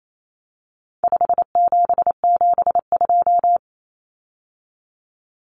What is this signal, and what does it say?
Morse code, heard as a keyed tone, about 28 wpm, 700 Hz. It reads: 5772
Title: Common Cut Numbers - 4-Digit 28wpm